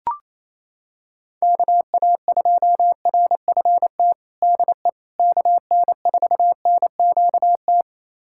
Morse code, heard as a keyed tone, about 28 wpm, 700 Hz. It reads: KA2RFT DE KN4NQT
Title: USA Callsign DE Callsign 28wpm